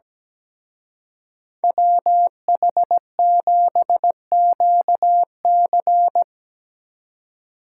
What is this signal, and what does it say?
Morse code, heard as a keyed tone, about 19 wpm, 700 Hz. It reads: WH7QC
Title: Speed Racing - USA 2x2 Call Signs 17wpm